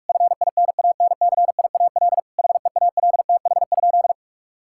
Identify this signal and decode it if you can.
Morse code, heard as a keyed tone, about 46 wpm, 700 Hz. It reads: FINANCIAL HEALTH?
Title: Sentences from Top 1000 Words - Review Part 5 - 45wpm